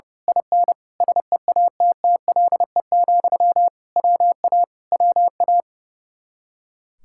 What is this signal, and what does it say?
Morse code, heard as a keyed tone, about 29 wpm, 700 Hz. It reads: IN SEATTLE, WA WA
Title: Speed-Racing Common QSO Phrases - 30wpm